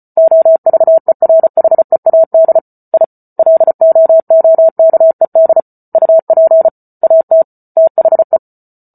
Morse code, heard as a keyed tone, about 34 wpm, 650 Hz. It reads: OVERHEAD I LOOKED UP AT THE